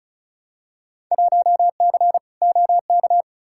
Morse code, heard as a keyed tone, about 34 wpm, 700 Hz. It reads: 1C OK
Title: Field Day Exchanges Encore - 35wpm